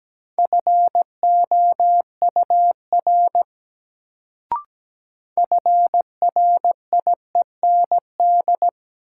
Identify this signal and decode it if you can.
Morse code, heard as a keyed tone, about 17 wpm, 700 Hz. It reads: FOUR FRIEND
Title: Sentences from Top 300 Words - New Word Drill - 17wpm